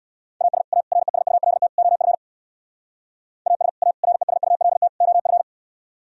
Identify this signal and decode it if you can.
Morse code, heard as a keyed tone, about 60 wpm, 700 Hz. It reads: AS A RESULT OF AS A RESULT OF
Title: Speed-Racing - Sets of 4 Words 50wpm